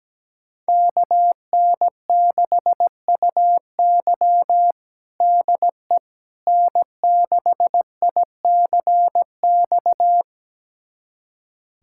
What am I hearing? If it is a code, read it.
Morse code, heard as a keyed tone, about 17 wpm, 700 Hz. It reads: KN6UY DE N6ICX
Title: USA Callsign DE Callsign - 5 Characters - 17wpm